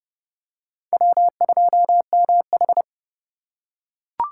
Morse code, heard as a keyed tone, about 26 wpm, 700 Hz. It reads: W2MH